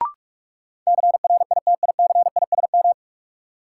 Morse code, heard as a keyed tone, about 46 wpm, 700 Hz. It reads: CRITICISM
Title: Sentences from Top 1300 Words - New Word Drill Alt - 45wpm